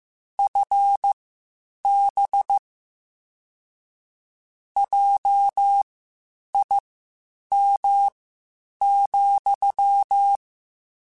Morse code, heard as a keyed tone, about 15 wpm, spaced as for 8 wpm, 800 Hz. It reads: FB JIM,
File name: (spaced) / QSOs Spaced-mp3s / WD0BGP 1 @13 sp